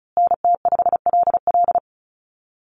Morse code, heard as a keyed tone, about 34 wpm, 700 Hz. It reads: NT5LL
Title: USA Call Signs Repeated 3x 35wpm